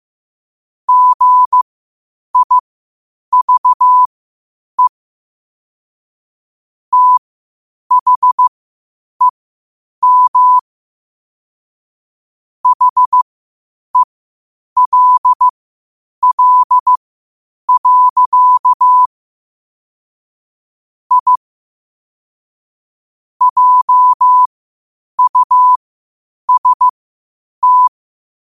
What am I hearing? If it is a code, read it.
Morse code, heard as a keyed tone, about 15 wpm, spaced as for 8 wpm, 1000 Hz. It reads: GIVE THEM HELL. I JUST
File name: qotdfiles / 8wpm